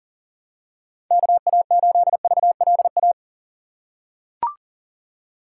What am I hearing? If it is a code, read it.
Morse code, heard as a keyed tone, about 39 wpm, 700 Hz. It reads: KA8VLA
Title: USA Call Signs - 6 Characters - 4x Character Spacing - 40wpm